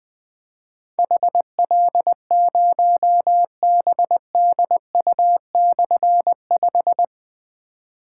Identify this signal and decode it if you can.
Morse code, heard as a keyed tone, about 21 wpm, 700 Hz. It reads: HL0BDU/5
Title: Difficult Call Signs 20wpm